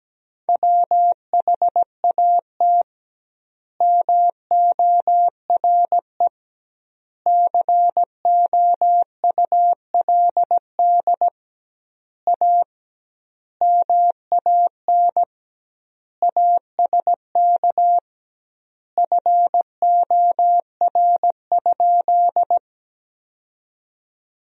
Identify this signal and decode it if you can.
Morse code, heard as a keyed tone, about 17 wpm, 700 Hz. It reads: WHAT MORE COULD A MAN ASK FOR?
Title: 2x Word-Spacing - Sentences from Top 200 Words - 17wpm